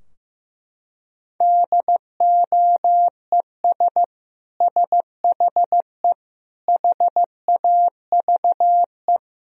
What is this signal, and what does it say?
Morse code, heard as a keyed tone, about 15 wpm, 700 Hz. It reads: DOES SHE HAVE
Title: Sentences from Top 1000 Words - Review Part 2 - 15wpm